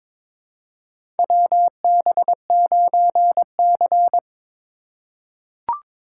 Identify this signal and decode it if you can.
Morse code, heard as a keyed tone, about 22 wpm, 700 Hz. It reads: WB9C
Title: USA Call Signs - 4 Characters - 4x Character Spacing - 22wpm